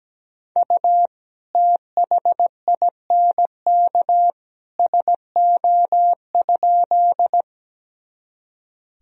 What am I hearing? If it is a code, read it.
Morse code, heard as a keyed tone, about 17 wpm, 700 Hz. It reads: U THINK SO?
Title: Sentences from Top 200 Words and Abbreviations 17wpm